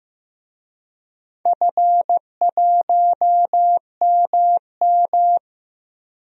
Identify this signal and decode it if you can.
Morse code, heard as a keyed tone, about 15 wpm, 700 Hz. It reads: F1MM
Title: DXCC-100 Call Signs - 4 Characters - 15wpm